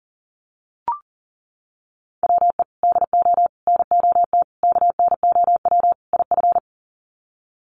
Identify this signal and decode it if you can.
Morse code, heard as a keyed tone, about 40 wpm, 700 Hz. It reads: WE DO NOT KNOW IF
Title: Sets of 5 Words Encore2 40wpm